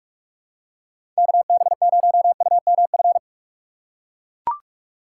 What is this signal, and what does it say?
Morse code, heard as a keyed tone, about 41 wpm, 700 Hz. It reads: KB0UMF
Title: USA Call Signs - 6 Characters - 3x Character Spacing - 45wpm